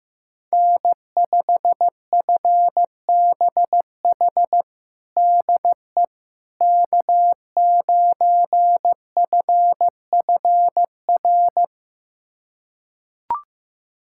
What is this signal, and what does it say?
Morse code, heard as a keyed tone, about 15 wpm, 700 Hz. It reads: N5FBH DE K9FFR
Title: USA Callsign DE Callsign - 5 Characters - 15wpm